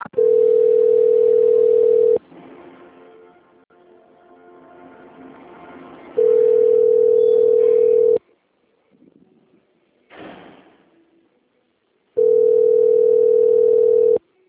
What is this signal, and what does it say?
Ringback tone (North America), 440 + 480 Hz.
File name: out- - - - - .13792